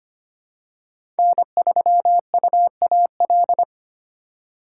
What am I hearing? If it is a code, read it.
Morse code, heard as a keyed tone, about 25 wpm, 700 Hz. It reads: N3UAL
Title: USA Call Signs - 5 Characters - 3x Character Spacing - 25wpm